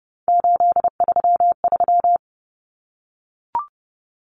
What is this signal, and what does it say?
Morse code, heard as a keyed tone, about 30 wpm, 700 Hz. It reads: 833
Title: Numbers - 3-Digit 30wpm